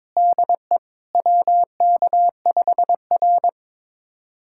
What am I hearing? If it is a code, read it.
Morse code, heard as a keyed tone, about 23 wpm, 700 Hz. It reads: DE WK5R
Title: USA Callsign DE Callsign - 4 Characters - 22wpm